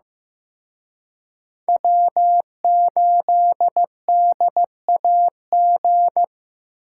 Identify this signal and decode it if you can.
Morse code, heard as a keyed tone, about 18 wpm, 700 Hz. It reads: W8DAG
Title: Speed Racing - USA 1x3 Call Signs 15wpm